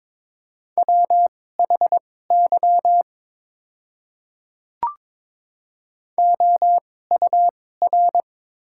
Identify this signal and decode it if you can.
Morse code, heard as a keyed tone, about 22 wpm, spaced as for 16 wpm, 700 Hz. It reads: WHY OUR
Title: 2x Character Spacing - Top 300 Words - 22wpm